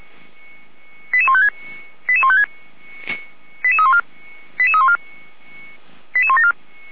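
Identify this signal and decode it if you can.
Five-tone selective call (ZVEI), heard as quick runs of five tones, tones of about 80 ms. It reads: 88156 88156 88314 88314 88174
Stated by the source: ZVEI five-tone selective call received by SDR.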